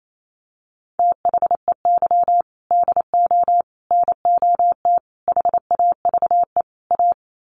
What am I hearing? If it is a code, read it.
Morse code, heard as a keyed tone, about 28 wpm, 700 Hz. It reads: THEY DO NOT HAVE A
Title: Sets of 5 Words 28wpm